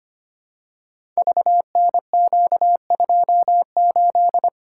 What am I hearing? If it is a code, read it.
Morse code, heard as a keyed tone, about 25 wpm, 700 Hz. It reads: VNQ28
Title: Code Groups - 5 Letter:Numbers - 25wpm